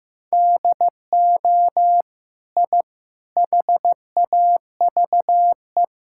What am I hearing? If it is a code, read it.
Morse code, heard as a keyed tone, about 15 wpm, 700 Hz. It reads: DO I HAVE
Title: Sentences from Top 1000 Words - Review Part 6 - 15wpm